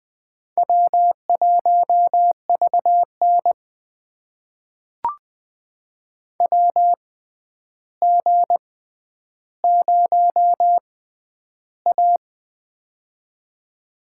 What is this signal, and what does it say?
Morse code, heard as a keyed tone, about 20 wpm, spaced as for 10 wpm, 700 Hz. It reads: W1VN WG0A
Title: USA Call Signs - 4 Characters - 6x Character Spacing - 20wpm